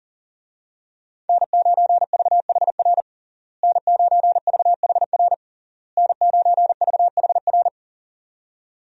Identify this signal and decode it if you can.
Morse code, heard as a keyed tone, about 40 wpm, 700 Hz. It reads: N9VHR N9VHR N9VHR
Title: USA Call Signs - 5 Characters - Repeated 3x - 40wpm